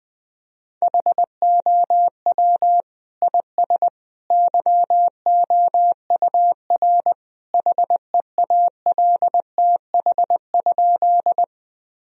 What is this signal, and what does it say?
Morse code, heard as a keyed tone, about 20 wpm, 700 Hz. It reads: HOW IS YOUR HEALTH?
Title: Sentences from Top 1000 Words - Review Part 3 - 20wpm